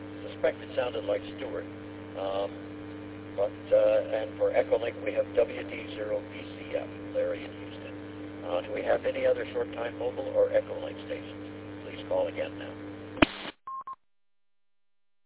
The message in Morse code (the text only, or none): none